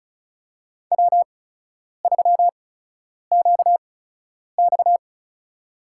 Morse code, heard as a keyed tone, about 36 wpm, spaced as for 10 wpm, 700 Hz. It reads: W3QX
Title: USA Call Signs - 8x Character Spacing - 35wpm